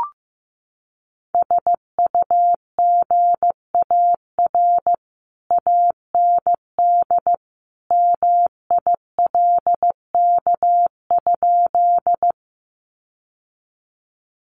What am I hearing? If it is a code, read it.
Morse code, heard as a keyed tone, about 15 wpm, 700 Hz. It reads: SUGAR AND MILK?
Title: Sentences from Top 1500 Words 15wpm